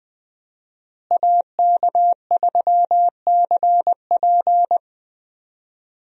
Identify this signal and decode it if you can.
Morse code, heard as a keyed tone, about 21 wpm, 700 Hz. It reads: AK3CP
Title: USA Call Signs - 5 Characters - Repeated 2x - 20wpm